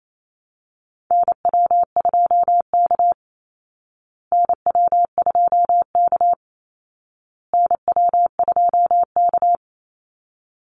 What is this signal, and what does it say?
Morse code, heard as a keyed tone, about 28 wpm, 700 Hz. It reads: NW2K NW2K NW2K
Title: Call Signs Repeated 2x - 28wpm